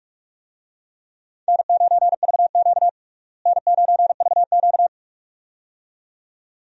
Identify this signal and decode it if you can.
Morse code, heard as a keyed tone, about 46 wpm, 700 Hz. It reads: N9VQ N9VQ
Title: USA Call Signs - 4 Characters - Repeated 2x - 45wpm